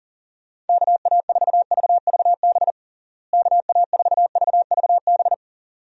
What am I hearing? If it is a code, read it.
Morse code, heard as a keyed tone, about 40 wpm, 700 Hz. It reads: KA4VVB KA4VVB
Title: USA Call Signs - 6 Characters - Repeated 2x - 40wpm